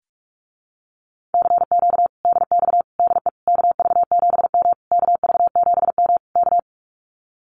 Morse code, heard as a keyed tone, about 44 wpm, 700 Hz. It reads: CQ DX DE KV7M KV7M K
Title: Common QSO Phrases Encore 45wpm